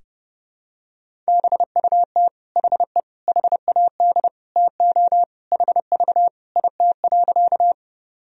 Morse code, heard as a keyed tone, about 30 wpm, 700 Hz. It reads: BUT HE HAD TO HV IT.
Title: Sentences from Top 100 Words and Abbreviations 30wpm